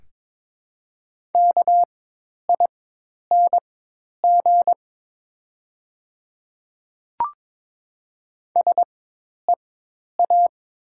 Morse code, heard as a keyed tone, about 22 wpm, spaced as for 10 wpm, 700 Hz. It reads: KING SEA
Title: 4x Character Spacing - Top 500 Words - 22wpm